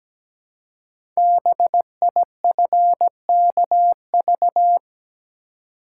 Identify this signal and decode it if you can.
Morse code, heard as a keyed tone, about 17 wpm, 700 Hz. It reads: BIFKV
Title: Code Groups - 5 Letter:Numbers - 17wpm